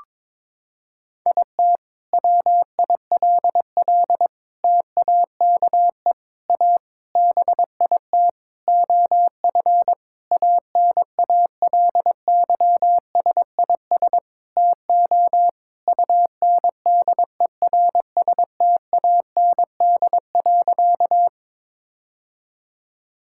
Morse code, heard as a keyed tone, about 22 wpm, 700 Hz. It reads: IT WILL TAKE A BIT OF ANALYSIS TO UNDERSTAND.
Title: Sentences from Top 1000 Words - Review Part 5 - 22wpm